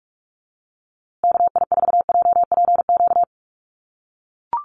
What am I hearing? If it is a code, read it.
Morse code, heard as a keyed tone, about 44 wpm, 700 Hz. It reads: KI4JPQ
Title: USA Call Signs Repeated 3x 45wpm